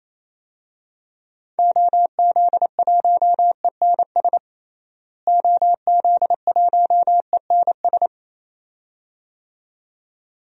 Morse code, heard as a keyed tone, about 28 wpm, 700 Hz. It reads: OZ1ENS OZ1ENS
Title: DXCC-100 Call Signs - 6 Characters - Repeated 2x - 28wpm